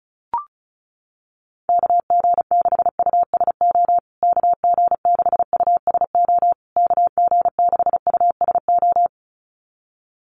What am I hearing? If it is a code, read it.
Morse code, heard as a keyed tone, about 34 wpm, 700 Hz. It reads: KG6USO KG6USO KG6USO
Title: USA Call Signs Repeated 3x 35wpm